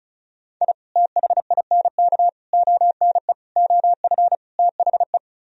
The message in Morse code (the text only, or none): I THINK ONE OF THE